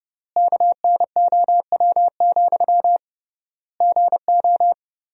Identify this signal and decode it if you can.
Morse code, heard as a keyed tone, about 32 wpm, 700 Hz. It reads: KNOW, GO
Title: 3x Word-Spacing - Sentences from Top 200 Words - 30wpm